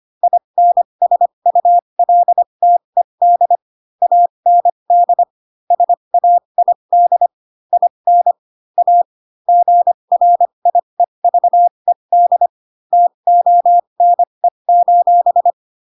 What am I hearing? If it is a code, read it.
Morse code, heard as a keyed tone, about 25 wpm, spaced as for 22 wpm, 700 Hz. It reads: INSULTED AND SAID IN A GRIEVED TONE:
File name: Pinocchio / Pinocchio ch3 18 25 ik1hge